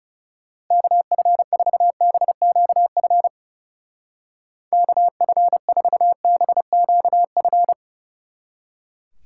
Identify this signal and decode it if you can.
Morse code, heard as a keyed tone, about 36 wpm, 700 Hz. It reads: KF4BQF KF4BQF
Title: Speed Racing - USA 2x3 Call Signs 30wpm